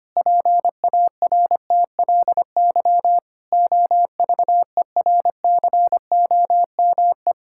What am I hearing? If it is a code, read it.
Morse code, heard as a keyed tone, about 25 wpm, 700 Hz. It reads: PARTLY OVERCOME
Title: Pangram Challenge v2 - 25wpm